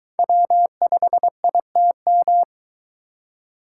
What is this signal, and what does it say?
Morse code, heard as a keyed tone, about 22 wpm, 700 Hz. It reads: W5ITM